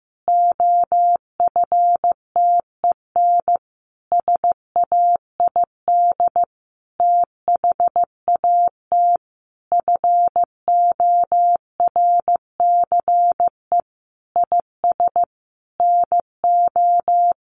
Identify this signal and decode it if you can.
Morse code, heard as a keyed tone, about 15 wpm, 700 Hz. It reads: OFTEN SAID THAT FORCE IS NO
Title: Sentences from Top 1400 Words 15wpm